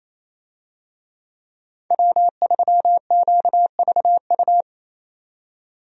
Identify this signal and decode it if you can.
Morse code, heard as a keyed tone, about 28 wpm, 700 Hz. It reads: W3QVU
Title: USA Call Signs - 5 Characters - 28wpm